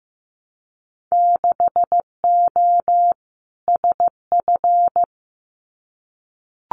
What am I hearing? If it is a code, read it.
Morse code, heard as a keyed tone, about 15 wpm, 700 Hz. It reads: 6O SF
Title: Winter Field Day Exchanges Encore 15wpm